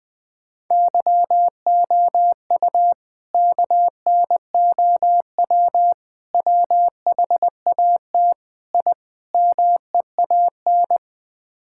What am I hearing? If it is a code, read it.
Morse code, heard as a keyed tone, about 21 wpm, 700 Hz. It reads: YOU KNOW WHAT I MEAN
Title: Build A Phrase - 5-Word Phrases - 20wpm